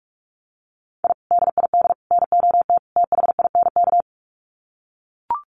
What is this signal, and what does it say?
Morse code, heard as a keyed tone, about 44 wpm, 700 Hz. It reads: I DID NOT THINK
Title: Sets of 4 Words 45wpm